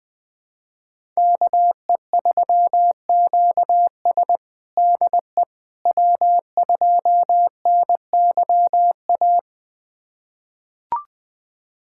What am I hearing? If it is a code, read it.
Morse code, heard as a keyed tone, about 20 wpm, 700 Hz. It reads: KE3QS DE W2NYA
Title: USA Callsign DE Callsign - 5 Characters - 20wpm